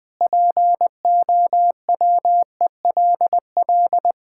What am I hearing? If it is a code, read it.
Morse code, heard as a keyed tone, about 20 wpm, 700 Hz. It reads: POWELL
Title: Pangram Challenge v2 - 20wpm